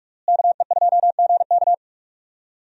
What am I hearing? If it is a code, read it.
Morse code, heard as a keyed tone, about 44 wpm, 700 Hz. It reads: KE2GK
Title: USA Call Signs - 5 Characters - Repeated 3x - 45wpm